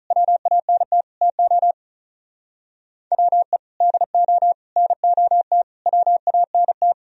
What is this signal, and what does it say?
Morse code, heard as a keyed tone, about 43 wpm, 700 Hz. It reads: WANT TO WE DO NOT WANT
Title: Speed-Racing - Sets of 5 Words 35wpm